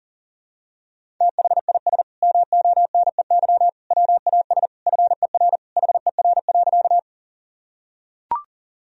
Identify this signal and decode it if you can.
Morse code, heard as a keyed tone, about 40 wpm, 700 Hz. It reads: THIS MONEY WAS FER HER.
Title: Sentences from Top 400 Words and Abbreviations 40wpm